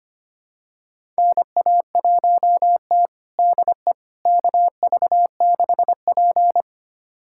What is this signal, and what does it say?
Morse code, heard as a keyed tone, about 25 wpm, 700 Hz. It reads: NA1T DE KV6P
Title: USA Callsign DE Callsign - 4 Characters - 25wpm